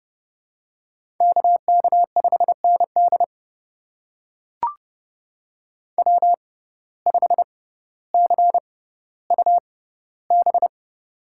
Morse code, heard as a keyed tone, about 31 wpm, spaced as for 16 wpm, 700 Hz. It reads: KK5ND W5CUB
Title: USA Call Signs - 5 Characters - 6x Character Spacing - 30wpm